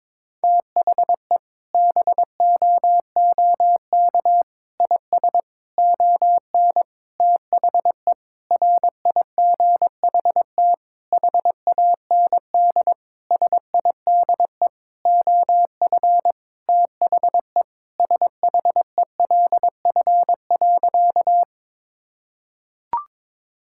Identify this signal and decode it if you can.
Morse code, heard as a keyed tone, about 22 wpm, 700 Hz. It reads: THE BOOK IS ON THE RIGHT HAND SIDE OF THE SHELF.
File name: Sentences with 7-Word Phrases - 22wpm